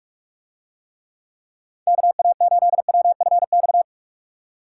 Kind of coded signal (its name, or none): Morse code